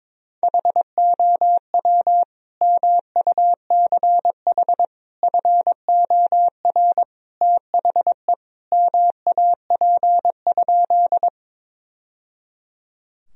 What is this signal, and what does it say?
Morse code, heard as a keyed tone, about 23 wpm, 700 Hz. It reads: HOW MUCH FOR THE MAP?
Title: Sentences from Top 1000 Words - Review Part 2 - 22wpm